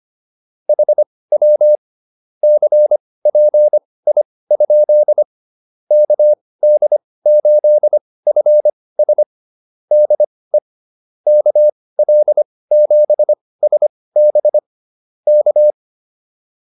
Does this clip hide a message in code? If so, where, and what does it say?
Yes: Morse code from 0.7 to 15.7 s, heard as a keyed tone, about 25 wpm, spaced as for 18 wpm, 600 Hz. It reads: HW CPI? KD8FS DE KL7SB K